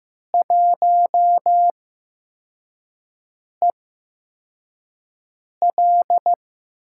Morse code, heard as a keyed tone, about 15 wpm, spaced as for 4 wpm, 700 Hz. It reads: 1EL
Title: USA Call Signs - 8x Character Spacing - 15wpm